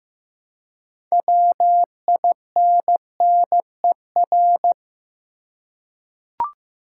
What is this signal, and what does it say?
Morse code, heard as a keyed tone, about 15 wpm, 700 Hz. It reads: WINNER
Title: Six Letter Words - 15wpm